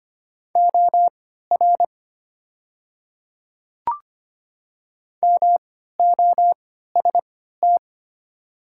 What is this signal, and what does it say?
Morse code, heard as a keyed tone, about 25 wpm, spaced as for 14 wpm, 700 Hz. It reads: OR MOST
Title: 3x Character Spacing - Top 100 Words - 25wpm